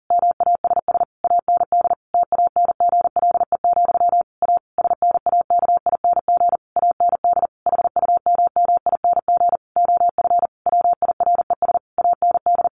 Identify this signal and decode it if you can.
Morse code, heard as a keyed tone, about 40 wpm, 700 Hz. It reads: MASS AND TANGLE, A SNAKING AND HUMMING OF WIRES AND
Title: A Sound Of Thunder 40wpm